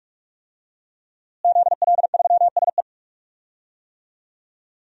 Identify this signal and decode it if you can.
Morse code, heard as a keyed tone, about 45 wpm, 700 Hz. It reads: ZL3SE
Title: DXCC-100 Call Signs - 5 Characters - Repeated 2x - 45wpm